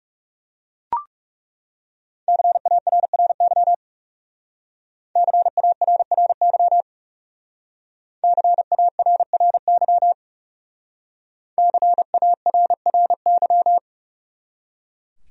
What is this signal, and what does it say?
Morse code, heard as a keyed tone, about 36 wpm, 700 Hz. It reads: CARRY CARRY CARRY CARRY
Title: Speed-Racing Top 300 Words 30wpm